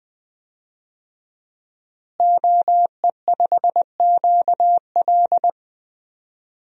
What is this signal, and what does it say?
Morse code, heard as a keyed tone, about 20 wpm, 700 Hz. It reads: OE5QL
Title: DXCC-100 Call Signs - 5 Characters - Repeated 2x - 20wpm